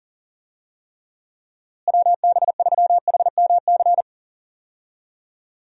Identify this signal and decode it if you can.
Morse code, heard as a keyed tone, about 40 wpm, 700 Hz. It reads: WB3HMC
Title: USA Call Signs - 6 Characters - Repeated 2x - 40wpm